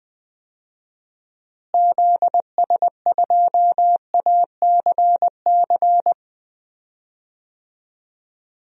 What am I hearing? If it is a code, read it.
Morse code, heard as a keyed tone, about 20 wpm, 700 Hz. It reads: ZS2ACC
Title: DXCC-100 Call Signs - 6 Characters - Repeated 3x - 20wpm